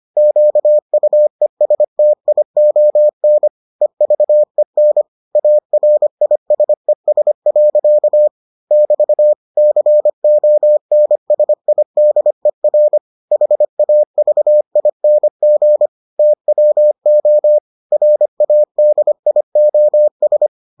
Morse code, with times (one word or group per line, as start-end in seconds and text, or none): 0.2-3.5 QUESTION
3.8-5.0 EVEN
5.4-8.3 ARISES.
8.7-9.3 <BT>
9.6-13.0 CONSIDER
13.3-15.9 HAVING
16.2-17.6 TWO
17.9-20.5 RADIOS